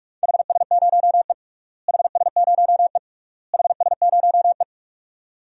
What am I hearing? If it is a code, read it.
Morse code, heard as a keyed tone, about 41 wpm, 700 Hz. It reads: HS0E HS0E HS0E
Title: DXCC-100 Call Signs - 4 Characters - Repeated 3x - 45wpm